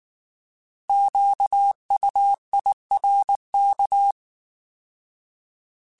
Morse code, heard as a keyed tone, about 19 wpm, 800 Hz. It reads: QUIRK
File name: Alphabetical / Alphabetical mp3s / Qs @18